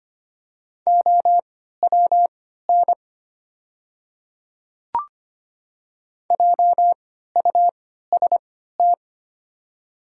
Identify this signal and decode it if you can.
Morse code, heard as a keyed tone, about 25 wpm, spaced as for 14 wpm, 700 Hz. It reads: OWN JUST
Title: 3x Character Spacing - Top 300 Words - 25wpm